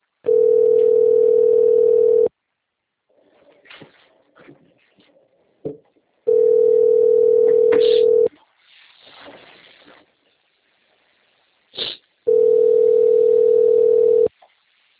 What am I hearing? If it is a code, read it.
Ringback tone (North America), 440 + 480 Hz.